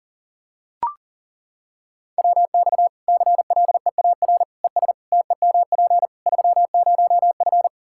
Morse code, heard as a keyed tone, about 40 wpm, 700 Hz. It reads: WX CLEAR ES TEMP 30F
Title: Practice QSO - Exchange 2 - Info - 40wpm